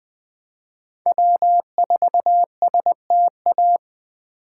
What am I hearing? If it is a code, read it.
Morse code, heard as a keyed tone, about 20 wpm, 700 Hz. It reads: W4STA
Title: USA Call Signs - 5 Characters - 3x Character Spacing - 20wpm